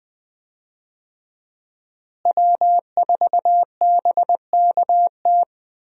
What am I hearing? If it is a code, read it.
Morse code, heard as a keyed tone, about 20 wpm, 700 Hz. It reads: W4BKT
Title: USA Call Signs - 5 Characters - Repeated 2x - 20wpm